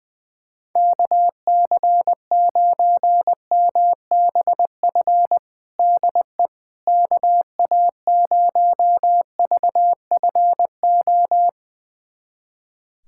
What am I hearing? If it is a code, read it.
Morse code, heard as a keyed tone, about 21 wpm, 700 Hz. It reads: KC9MBF DE KA0VFO
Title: USA Callsign DE Callsign - 6 Characters - 20wpm